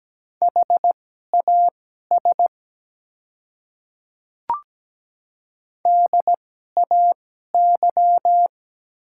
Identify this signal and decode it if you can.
Morse code, heard as a keyed tone, about 17 wpm, spaced as for 12 wpm, 700 Hz. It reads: HAS DAY
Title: 2x Character Spacing - Top 100 Words - 17wpm